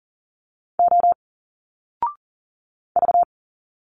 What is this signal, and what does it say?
Morse code, heard as a keyed tone, about 40 wpm, 700 Hz. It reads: O V
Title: Single Letters 40wpm